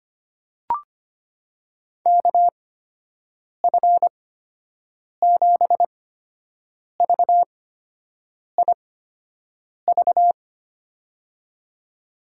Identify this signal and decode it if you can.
Morse code, heard as a keyed tone, about 25 wpm, spaced as for 7 wpm, 700 Hz. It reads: KF7VIV
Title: USA Call Signs - 6 Characters - 8x Character Spacing - 25wpm